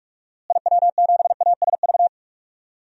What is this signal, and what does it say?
Morse code, heard as a keyed tone, about 41 wpm, 700 Hz. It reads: IW7ASV
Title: DXCC-100 Call Signs - 6 Characters - Repeated 2x - 45wpm